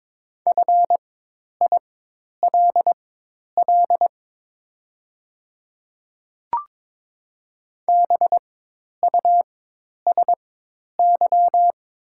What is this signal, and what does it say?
Morse code, heard as a keyed tone, about 22 wpm, spaced as for 10 wpm, 700 Hz. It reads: FILL BUSY
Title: 4x Character Spacing - Top 500 Words - 22wpm